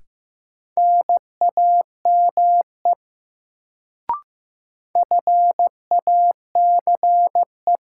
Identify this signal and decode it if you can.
Morse code, heard as a keyed tone, about 15 wpm, 700 Hz. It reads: NAME FACE
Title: Sets of 3 Words - New Word Drill - 15wpm